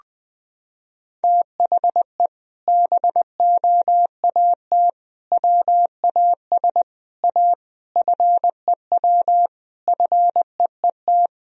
Morse code, heard as a keyed tone, about 20 wpm, 700 Hz. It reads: THE BOAT WAS A FEW FEET
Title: Sentences from Top 1000 Words - Review Part 3 - 20wpm